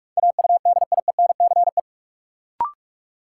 Morse code, heard as a keyed tone, about 42 wpm, 700 Hz. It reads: AUDIENCE
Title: Sentences from Top 800 Words - New Word Drill - 45wpm